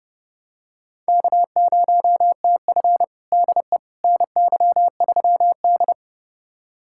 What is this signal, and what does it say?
Morse code, heard as a keyed tone, about 32 wpm, 700 Hz. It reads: K0TF DE NY3D
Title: USA Callsign DE Callsign - 4 Characters - 30wpm